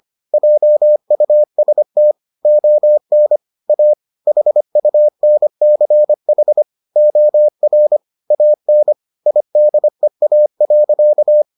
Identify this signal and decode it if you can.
Morse code, heard as a keyed tone, about 25 wpm, 600 Hz. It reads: JUST ON A HUNCH OR AN IDEA.